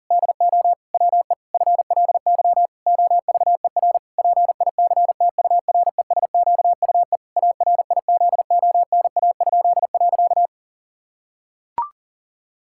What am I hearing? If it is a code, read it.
Morse code, heard as a keyed tone, about 40 wpm, 700 Hz. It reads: DO WE FLY OVER PICTURESQUE ARIZONA?.
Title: Pangram Challenge v2 - 40wpm